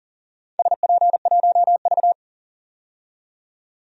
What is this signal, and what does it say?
Morse code, heard as a keyed tone, about 40 wpm, 700 Hz. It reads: SP1V
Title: DXCC-100 Call Signs - 4 Characters - 40wpm